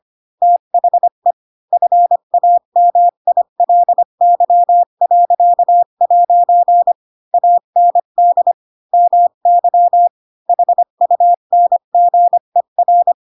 Morse code, heard as a keyed tone, about 25 wpm, 700 Hz. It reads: THE FAMILY.' AND MY HUNGER
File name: Pinocchio / Pinocchio ch7 20 25 ik1hge